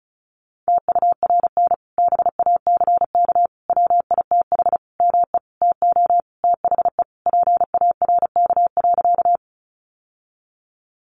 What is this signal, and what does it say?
Morse code, heard as a keyed tone, about 34 wpm, 700 Hz. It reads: TURN BACK WITH ME TO THE PARK.
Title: Sentences from Top 1300 Words 35wpm